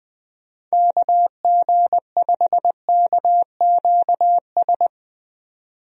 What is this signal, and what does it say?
Morse code, heard as a keyed tone, about 21 wpm, 700 Hz. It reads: KG5KQS
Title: USA Call Signs - 6 Characters - 20wpm